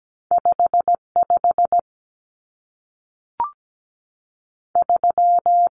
Morse code, heard as a keyed tone, about 17 wpm, 700 Hz. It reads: 5 5 3
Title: Numbers - 2-Digit 17wpm